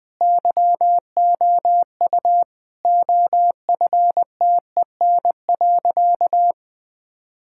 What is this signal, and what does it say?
Morse code, heard as a keyed tone, about 21 wpm, 700 Hz. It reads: YOU OFTEN.
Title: Sentences from Top 1000 Words - Review Part 6 - 20wpm